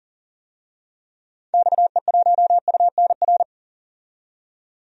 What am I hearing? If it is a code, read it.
Morse code, heard as a keyed tone, about 40 wpm, 700 Hz. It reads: XE1UNR
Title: DXCC-100 Call Signs - 6 Characters - 40wpm